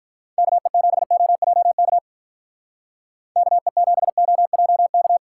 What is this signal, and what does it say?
Morse code, heard as a keyed tone, about 44 wpm, 700 Hz. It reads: KE7OJK KE7OJK
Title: Speed Racing - USA 2x3 Call Signs 35wpm